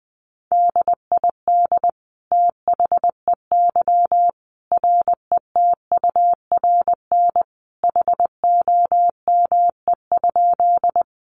Morse code, heard as a keyed tone, about 20 wpm, 700 Hz. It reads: DID THEY RETURN HOME?
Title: Sentences from Top 800 Words 20wpm